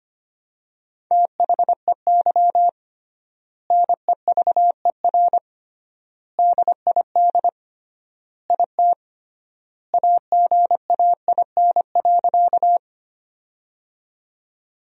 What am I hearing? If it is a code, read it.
Morse code, heard as a keyed tone, about 25 wpm, 700 Hz. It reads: THEY NEVER DID IT AGAIN.
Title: 3x Word-Spacing - Sentences from Top 300 Words - 25wpm